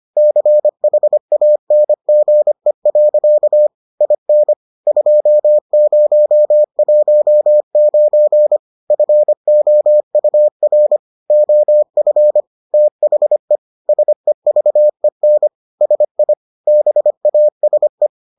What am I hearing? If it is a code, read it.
Morse code, heard as a keyed tone, about 25 wpm, 600 Hz. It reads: CHANGE. IN 2019 FOUR OF THE SEVEN SI BASE